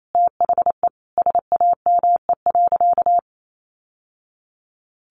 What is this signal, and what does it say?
Morse code, heard as a keyed tone, about 28 wpm, 700 Hz. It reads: THE SAME.
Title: Sentences from Top 1000 Words 28wpm